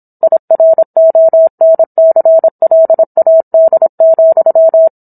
Morse code, heard as a keyed tone, about 27 wpm, 650 Hz. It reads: IRONCLAD,